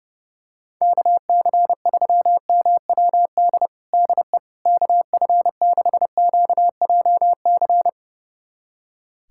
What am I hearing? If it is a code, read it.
Morse code, heard as a keyed tone, about 30 wpm, 700 Hz. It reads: KC3MWD DE KF6QJC
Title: USA Callsign DE Callsign 30wpm